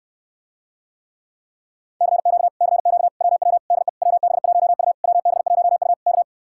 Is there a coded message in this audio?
Yes: Morse code from 2.0 to 6.2 s, heard as a keyed tone, about 67 wpm, 700 Hz. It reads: CQ CQ WW DE WB1U WB1U K